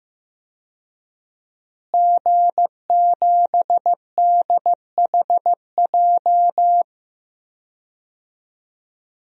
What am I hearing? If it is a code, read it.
Morse code, heard as a keyed tone, about 15 wpm, 700 Hz. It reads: G7DHJ
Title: DXCC-100 Call Signs - 5 Characters - Repeated 3x - 15wpm